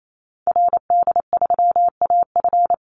Morse code, heard as a keyed tone, about 28 wpm, 700 Hz. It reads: RD3AF
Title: Call Signs Repeated 2x - 28wpm